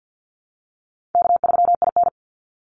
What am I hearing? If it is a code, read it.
Morse code, heard as a keyed tone, about 48 wpm, 700 Hz. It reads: K3IN
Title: Easy Call Signs Repeated 3x - 50wpm